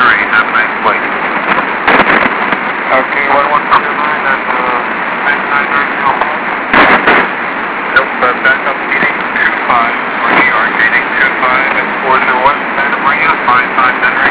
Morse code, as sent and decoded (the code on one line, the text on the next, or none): none